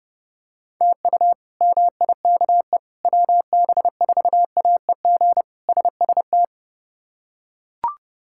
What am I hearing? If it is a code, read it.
Morse code, heard as a keyed tone, about 30 wpm, 700 Hz. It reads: TU MIKE WB4AEG SST
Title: K1USN SST QSO Practice - 30wpm